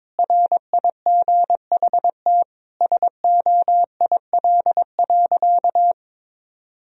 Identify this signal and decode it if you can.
Morse code, heard as a keyed tone, about 22 wpm, 700 Hz. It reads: RIGHT SOIL.
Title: Sentences from Top 1000 Words - Review Part 4 - 22wpm